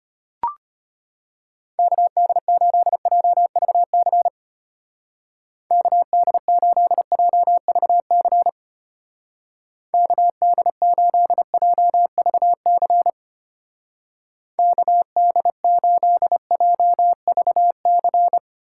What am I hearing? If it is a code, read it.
Morse code, heard as a keyed tone, about 32 wpm, 700 Hz. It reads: KD8JVC KD8JVC KD8JVC KD8JVC